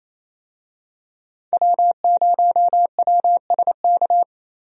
Morse code, heard as a keyed tone, about 28 wpm, 700 Hz. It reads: W0WSK
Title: USA Call Signs - 5 Characters - 28wpm